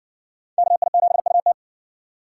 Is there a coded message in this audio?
Yes: Morse code from 0.6 to 1.5 s, heard as a keyed tone, about 60 wpm, 700 Hz. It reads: KI7UT